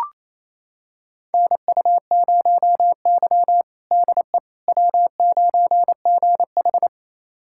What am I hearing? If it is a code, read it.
Morse code, heard as a keyed tone, about 28 wpm, 700 Hz. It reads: NU0Y DE W9GH
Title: USA Callsign DE Callsign - 4 Characters - 28wpm